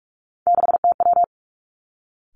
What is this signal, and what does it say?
Morse code, heard as a keyed tone, about 46 wpm, 700 Hz. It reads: 6TW
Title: Three Challenging Letter-Number Combinations 45wpm